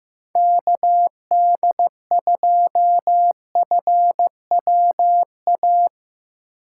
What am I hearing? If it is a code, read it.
Morse code, heard as a keyed tone, about 15 wpm, 700 Hz. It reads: KD2FWA
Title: USA Call Signs - 4x Character Spacing - 15wpm